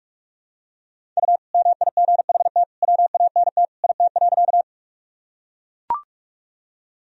Morse code, heard as a keyed tone, about 41 wpm, 700 Hz. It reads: U MIGHT WANT IT.
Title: Sentences from Top 300 Words and Abbreviations 45wpm